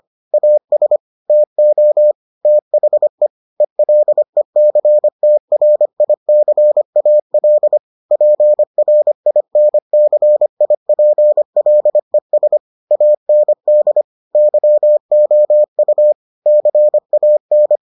Morse code, heard as a keyed tone, about 25 wpm, 600 Hz. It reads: AS TO THE ELECTRICAL PRINCIPLES AND YOU CAN